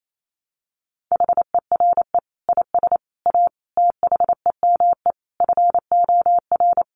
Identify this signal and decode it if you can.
Morse code, heard as a keyed tone, about 28 wpm, 700 Hz. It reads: HERE IS A THEME FOR
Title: Sentences from Top 1100 Words 28wpm